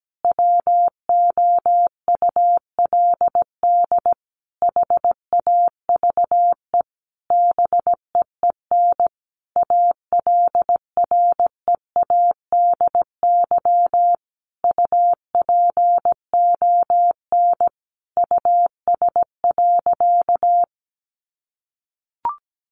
Morse code, heard as a keyed tone, about 17 wpm, 700 Hz. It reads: WOULD HAVE BEEN ALREADY UPON US.
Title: Sentences from Top 1500 Words 17wpm